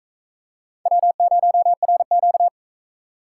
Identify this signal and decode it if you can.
Morse code, heard as a keyed tone, about 42 wpm, 700 Hz. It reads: W0RQ